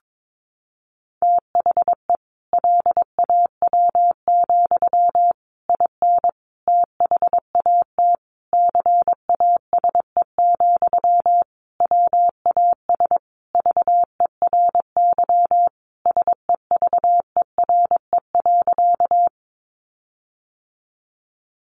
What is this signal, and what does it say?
Morse code, heard as a keyed tone, about 22 wpm, 700 Hz. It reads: THE LAW, IN THAT CASE, WAS VERY SEVERE.
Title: Sentences from Top 1400 Words 22wpm